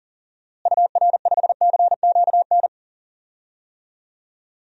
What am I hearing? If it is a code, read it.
Morse code, heard as a keyed tone, about 40 wpm, 700 Hz. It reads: UR5CQN
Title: DXCC-100 Call Signs - 6 Characters - Repeated 3x - 40wpm